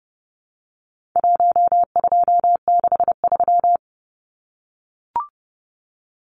Morse code, heard as a keyed tone, about 28 wpm, 700 Hz. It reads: 1263
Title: Numbers - 4-Digit 30wpm